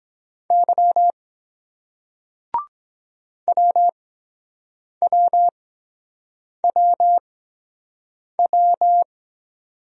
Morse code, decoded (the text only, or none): Y WWWW